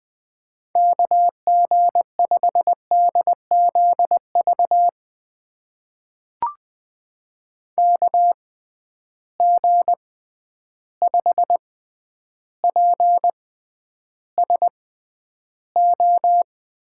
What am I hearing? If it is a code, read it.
Morse code, heard as a keyed tone, about 20 wpm, spaced as for 10 wpm, 700 Hz. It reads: KG5DZV KG5PSO
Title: USA Call Signs - 6 Characters - 6x Character Spacing - 20wpm